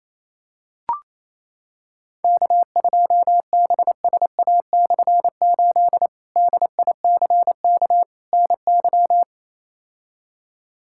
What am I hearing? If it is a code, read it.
Morse code, heard as a keyed tone, about 28 wpm, 700 Hz. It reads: K2BSA/8 DICK NY
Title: K1USN SST QSO Practice - 28wpm